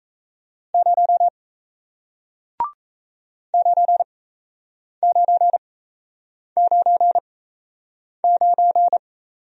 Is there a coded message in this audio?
Yes: Morse code from 0.7 to 9.0 s, heard as a keyed tone, about 37 wpm, spaced as for 8 wpm, 700 Hz. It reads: 0 9999